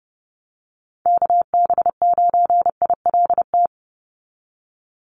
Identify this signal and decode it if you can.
Morse code, heard as a keyed tone, about 30 wpm, 700 Hz. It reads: KB9ILT
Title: NA SOTA Chaser Callsigns 30wpm